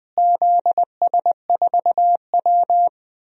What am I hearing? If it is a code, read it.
Morse code, heard as a keyed tone, about 20 wpm, 700 Hz. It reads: ZS4W
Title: DXCC-100 Call Signs - 4 Characters - Repeated 3x - 20wpm